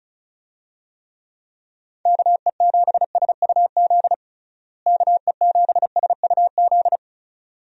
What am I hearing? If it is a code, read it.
Morse code, heard as a keyed tone, about 36 wpm, 700 Hz. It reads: KE7SUZ KE7SUZ
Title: USA Call Signs - 6 Characters - Repeated 2x - 35wpm